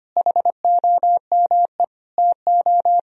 Morse code, heard as a keyed tone, about 25 wpm, 700 Hz. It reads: HOME TO
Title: Sentences from Top 1000 Words - Review Part 4 - 25wpm